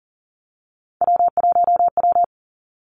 Morse code, heard as a keyed tone, about 42 wpm, 700 Hz. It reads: W1W
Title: Three Challenging Letter-Number Combinations 40wpm